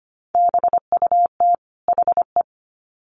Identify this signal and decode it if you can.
Morse code, heard as a keyed tone, about 25 wpm, 700 Hz. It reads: BUT HE
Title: Sets of 2 Words 25wpm